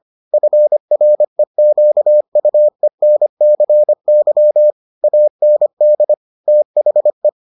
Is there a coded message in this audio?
Yes: Morse code from 0.3 to 7.3 s, heard as a keyed tone, about 25 wpm, 600 Hz. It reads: FREQUENCY AND THE